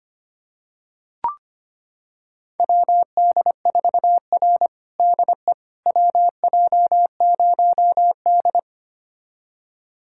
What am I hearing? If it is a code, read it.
Morse code, heard as a keyed tone, about 25 wpm, 700 Hz. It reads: WD4R DE WJ0D
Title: USA Callsign DE Callsign - 4 Characters - 25wpm